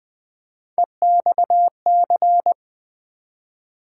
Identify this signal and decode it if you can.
Morse code, heard as a keyed tone, about 20 wpm, 700 Hz. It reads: EXC
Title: Build A Word - 3 Letters At A Time - 9-Letter Words - 20wpm